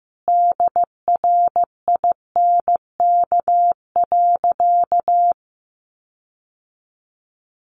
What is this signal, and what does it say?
Morse code, heard as a keyed tone, about 15 wpm, 700 Hz. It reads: DRINK.
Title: Sentences from Top 1500 Words 15wpm